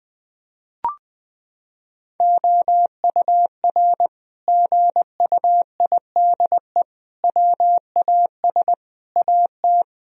Morse code, heard as a keyed tone, about 21 wpm, 700 Hz. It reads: OUR GUIDE WAS AT
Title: Sentences from Top 1000 Words - Review Part 3 - 20wpm